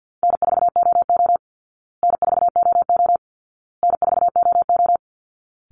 Morse code, heard as a keyed tone, about 50 wpm, 700 Hz. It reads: N4OO N4OO N4OO
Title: Call Signs Repeated 2x - 50wpm